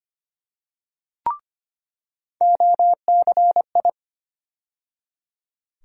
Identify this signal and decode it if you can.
Morse code, heard as a keyed tone, about 25 wpm, 700 Hz. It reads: OCI
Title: USA Call Sign Postfixes - 3 Letters 25wpm